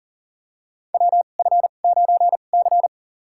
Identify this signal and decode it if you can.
Morse code, heard as a keyed tone, about 40 wpm, spaced as for 29 wpm, 700 Hz. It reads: WF9C
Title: USA Call Signs - 4 Characters - 2x Character Spacing - 40wpm